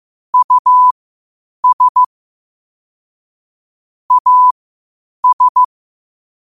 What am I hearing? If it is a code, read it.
Morse code, heard as a keyed tone, about 15 wpm, spaced as for 8 wpm, 1000 Hz. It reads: US AS
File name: 8wpm